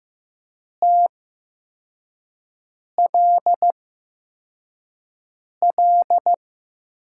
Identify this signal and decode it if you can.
Morse code, heard as a keyed tone, about 15 wpm, spaced as for 4 wpm, 700 Hz. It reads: TLL
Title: USA Call Signs - 6 Characters - 8x Character Spacing - 15wpm